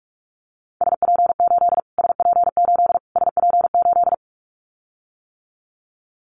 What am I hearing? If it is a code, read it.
Morse code, heard as a keyed tone, about 44 wpm, 700 Hz. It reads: SP8 SP8 SP8
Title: Callsign Prefixes for Top 25 DXCC - Repeated 3x - 45wpm